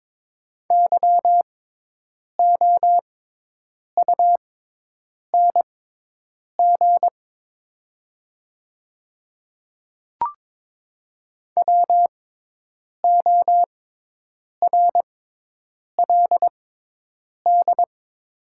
Morse code, heard as a keyed tone, about 22 wpm, spaced as for 8 wpm, 700 Hz. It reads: YOUNG WORLD
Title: 6x Character Spacing - Top 500 Words - 22wpm